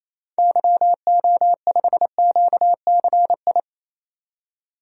Morse code, heard as a keyed tone, about 28 wpm, 700 Hz. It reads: YO5QCI
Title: DXCC-100 Call Signs - 6 Characters - 28wpm